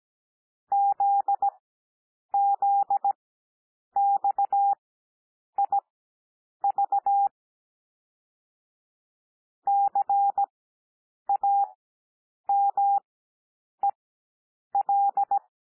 Morse code, heard as a keyed tone, about 17 wpm, spaced as for 8 wpm, 800 Hz. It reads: ZZXIV CAMEL
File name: cwmp3 / L2 SOUND LETTERS 17 wpm